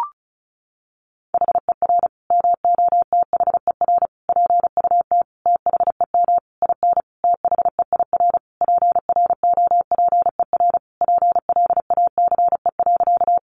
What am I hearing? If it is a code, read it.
Morse code, heard as a keyed tone, about 34 wpm, 700 Hz. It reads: HER MOTHER PUT THEM IN THEIR PROPER PLACE.
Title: Sentences from Top 800 Words 35wpm